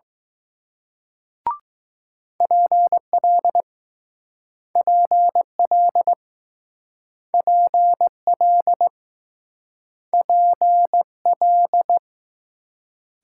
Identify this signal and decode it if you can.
Morse code, heard as a keyed tone, about 20 wpm, 700 Hz. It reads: PL PL PL PL